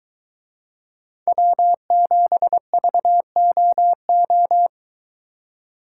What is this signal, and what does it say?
Morse code, heard as a keyed tone, about 22 wpm, 700 Hz. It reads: W7VOO